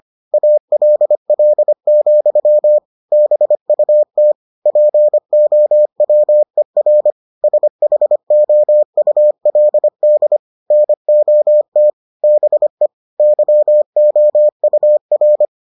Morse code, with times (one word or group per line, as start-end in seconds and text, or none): 0.3-2.8 ALL,
3.1-4.3 BUT
4.7-7.1 POWER
7.4-10.4 SHOULD
10.7-11.9 NOT
12.2-12.9 BE
13.2-15.4 YOUR